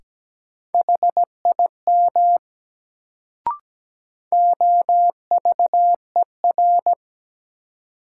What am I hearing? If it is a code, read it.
Morse code, heard as a keyed tone, about 17 wpm, 700 Hz. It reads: HIM OVER
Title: Sentences from Top 100 Words - New Word Drill - 17wpm